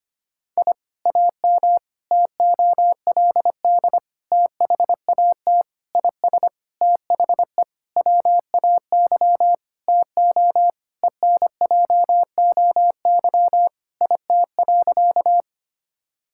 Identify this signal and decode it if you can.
Morse code, heard as a keyed tone, about 25 wpm, 700 Hz. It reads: I AM TOLD THAT IS THE WAY TO ENJOY IT.
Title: Sentences from Top 1000 Words - Review Part 5 - 25wpm